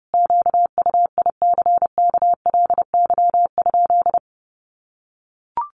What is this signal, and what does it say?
Morse code, heard as a keyed tone, about 30 wpm, 700 Hz. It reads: QUICKLY?
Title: Sentences from Top 700 Words 30wpm